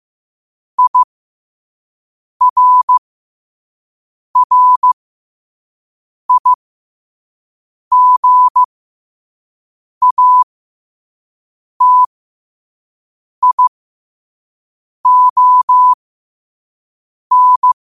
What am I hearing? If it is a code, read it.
Morse code, heard as a keyed tone, about 15 wpm, spaced as for 5 wpm, 1000 Hz. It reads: IRRIGATION
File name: qotdfiles / 5wpm